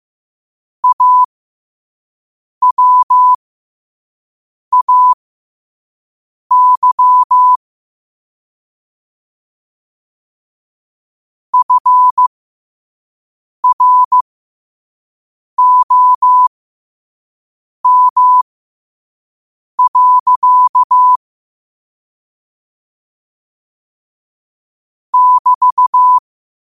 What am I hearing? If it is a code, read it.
Morse code, heard as a keyed tone, about 15 wpm, spaced as for 5 wpm, 1000 Hz. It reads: AWAY FROM. <BT>
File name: 5wpm